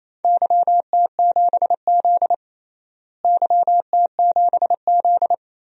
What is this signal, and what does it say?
Morse code, heard as a keyed tone, about 27 wpm, 700 Hz. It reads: YT7Z YT7Z
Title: DXCC-100 Call Signs - 4 Characters - Repeated 2x - 28wpm